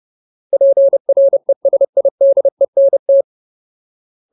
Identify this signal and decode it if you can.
Morse code, heard as a keyed tone, about 30 wpm, 550 Hz. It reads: PRESIDENT
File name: Ultimate Word List / 30WPM Ultimate